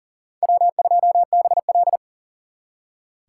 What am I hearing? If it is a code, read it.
Morse code, heard as a keyed tone, about 51 wpm, 700 Hz. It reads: W2BL